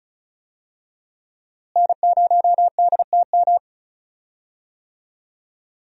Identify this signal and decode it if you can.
Morse code, heard as a keyed tone, about 36 wpm, 700 Hz. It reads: N0DTM
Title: USA Call Signs - 5 Characters - Repeated 3x - 35wpm